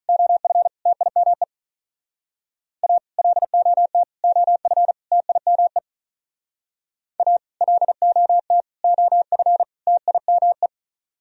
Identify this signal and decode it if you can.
Morse code, heard as a keyed tone, about 44 wpm, 700 Hz. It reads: OF TIME A LOT OF TIME A LOT OF TIME